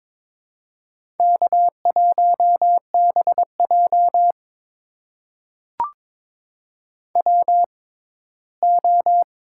Morse code, heard as a keyed tone, about 22 wpm, spaced as for 15 wpm, 700 Hz. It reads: K1BJ WO
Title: USA Call Signs - 4 Characters - 6x Character Spacing - 22wpm